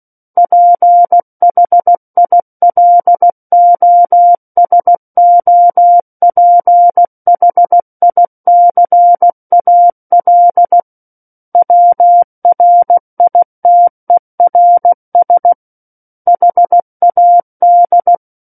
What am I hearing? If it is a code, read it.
Morse code, heard as a keyed tone, about 16 wpm, 700 Hz. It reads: PHILOSOPHICAL WRITERS HAD
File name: Book One / 07-Chapter 7 - 16 WPM